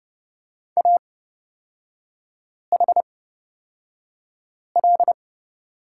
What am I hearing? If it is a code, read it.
Morse code, heard as a keyed tone, about 30 wpm, spaced as for 5 wpm, 690 Hz. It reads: AHL
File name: Lesson 37e - Letter x - Three Characters 30wpm